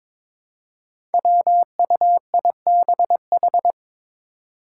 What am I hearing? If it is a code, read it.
Morse code, heard as a keyed tone, about 22 wpm, 700 Hz. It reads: WUIBH
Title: Five Challenging Letter-Number Combinations 22wpm